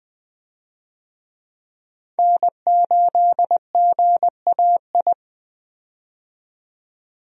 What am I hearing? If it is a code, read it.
Morse code, heard as a keyed tone, about 20 wpm, 700 Hz. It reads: N8GAI
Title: USA Call Signs - 5 Characters - 20wpm